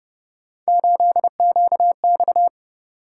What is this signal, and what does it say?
Morse code, heard as a keyed tone, about 29 wpm, 700 Hz. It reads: 8QX
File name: Speed-Racing - Three Letter-Number Combinations - 25wpm